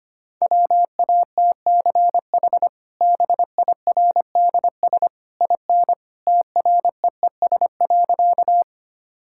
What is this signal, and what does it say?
Morse code, heard as a keyed tone, about 25 wpm, 700 Hz. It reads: WATCH BIRDS IN TREES.
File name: Easy Sentences with Hints - 25wpm